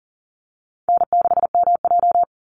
Morse code, heard as a keyed tone, about 40 wpm, 700 Hz. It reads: N6MJ
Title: Easy Call Signs Repeated 3x - 40wpm